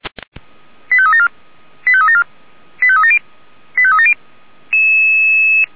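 Five-tone selective call (ZVEI), heard as quick runs of five tones, tones of about 70 ms. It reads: 85374 85374 85370 85370
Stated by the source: Five-tone ZVEI selective call received by SDR.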